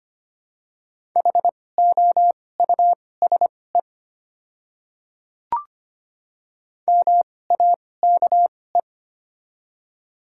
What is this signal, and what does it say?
Morse code, heard as a keyed tone, about 25 wpm, spaced as for 18 wpm, 700 Hz. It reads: HOUSE MAKE
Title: 2x Character Spacing - Top 300 Words - 25wpm